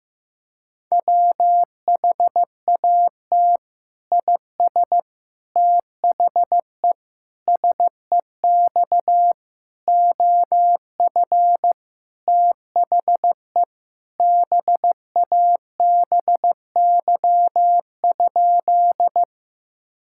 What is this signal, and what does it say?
Morse code, heard as a keyed tone, about 15 wpm, 700 Hz. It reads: WHAT IS THE SEX OF THE BABY?
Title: Sentences from Top 1600 Words 15wpm